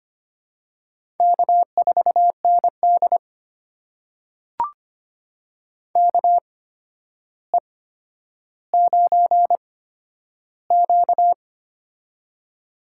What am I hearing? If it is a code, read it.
Morse code, heard as a keyed tone, about 25 wpm, spaced as for 11 wpm, 700 Hz. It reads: K4ND KE9Q
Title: USA Call Signs - 4 Characters - 8x Character Spacing - 25wpm